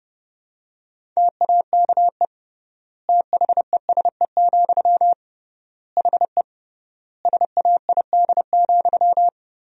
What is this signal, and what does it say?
Morse code, heard as a keyed tone, about 30 wpm, 700 Hz. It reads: TAKE THESE, HE SAID,
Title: 3x Word-Spacing - Sentences from Top 300 Words - 30wpm